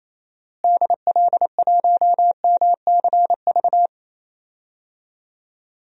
Morse code, heard as a keyed tone, about 28 wpm, 700 Hz. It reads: DL1MCV
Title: DXCC-100 Call Signs - 6 Characters - Repeated 3x - 28wpm